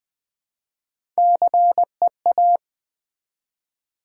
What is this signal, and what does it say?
Morse code, heard as a keyed tone, about 20 wpm, 700 Hz. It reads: CEA
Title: Build A Word - 1 Letter At A Time - 5-Letter Words - 20wpm